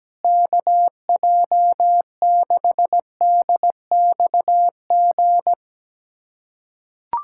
Morse code, heard as a keyed tone, about 17 wpm, 700 Hz. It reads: KJ6DXG
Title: USA Callsign DE Callsign 17wpm